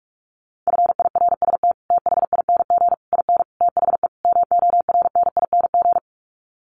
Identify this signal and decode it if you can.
Morse code, heard as a keyed tone, about 44 wpm, 700 Hz. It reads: FIRST THING IN THE MORNING
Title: Sets of 5 Words Encore2 45wpm